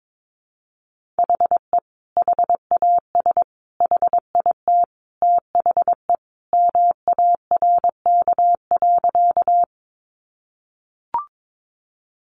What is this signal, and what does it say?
Morse code, heard as a keyed tone, about 22 wpm, 700 Hz. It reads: HE HAS HIT THE MARK.
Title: Sentences from Top 1200 Words 22wpm